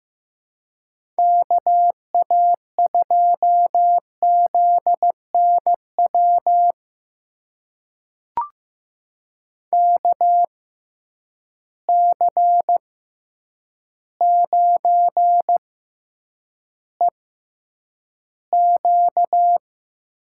Morse code, heard as a keyed tone, about 15 wpm, spaced as for 8 wpm, 700 Hz. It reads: KA2ZNW KC9EQ
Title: USA Call Signs - 6 Characters - 6x Character Spacing - 15wpm